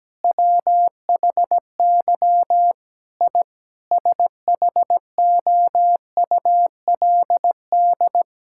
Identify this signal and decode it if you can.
Morse code, heard as a keyed tone, about 17 wpm, 700 Hz. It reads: WHY I SHOULD
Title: Sentences from Top 1000 Words - Review Part 6 - 17wpm